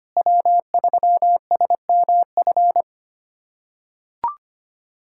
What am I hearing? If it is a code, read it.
Morse code, heard as a keyed tone, about 25 wpm, 700 Hz. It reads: W3SMF
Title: USA Callsign DE Callsign - 5 Characters - 25wpm